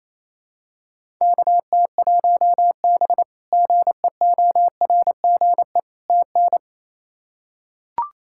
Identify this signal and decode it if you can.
Morse code, heard as a keyed tone, about 27 wpm, 700 Hz. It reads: KT1B GEORGE TN
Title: K1USN SST QSO Practice - 28wpm